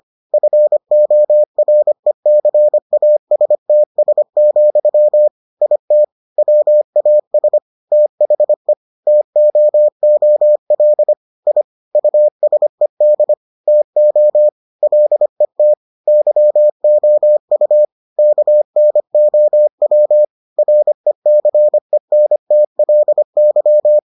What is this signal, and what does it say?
Morse code, heard as a keyed tone, about 25 wpm, 600 Hz. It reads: FORECASTS, IT WAS THE TOOL I USED TO LET YOU KNOW RECENTLY